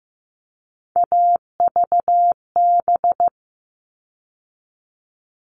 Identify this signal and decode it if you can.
Morse code, heard as a keyed tone, about 15 wpm, 700 Hz. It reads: AVB
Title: All Cut Numbers - 3-Digit 15wpm